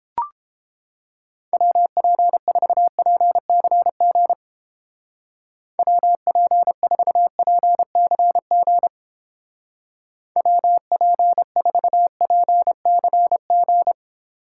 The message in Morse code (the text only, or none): WP4PCG WP4PCG WP4PCG